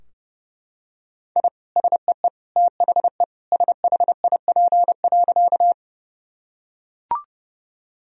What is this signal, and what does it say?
Morse code, heard as a keyed tone, about 30 wpm, 700 Hz. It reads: I SEE THE SHIP.
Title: Sentences from Top 1000 Words - Review Part 2 - 30wpm